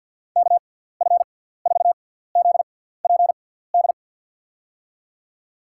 Morse code, heard as a keyed tone, about 53 wpm, spaced as for 17 wpm, 700 Hz. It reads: KF4ZPD
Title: USA Call Signs - 6 Characters - 6x Character Spacing - 50wpm